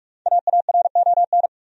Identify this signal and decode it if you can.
Morse code, heard as a keyed tone, about 46 wpm, 700 Hz. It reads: AARON
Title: USA Common Names - Max 5 Letters - 45wpm